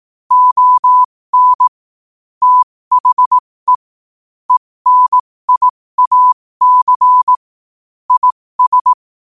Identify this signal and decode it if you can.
Morse code, heard as a keyed tone, about 18 wpm, spaced as for 15 wpm, 1000 Hz. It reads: ON THE ENIAC IS
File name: qotdfiles / 15wpm